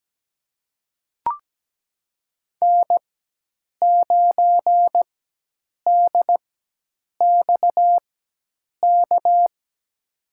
Morse code, heard as a keyed tone, about 17 wpm, spaced as for 8 wpm, 700 Hz. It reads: N9DXK
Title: USA Call Signs - 5 Characters - 4x Character Spacing - 17wpm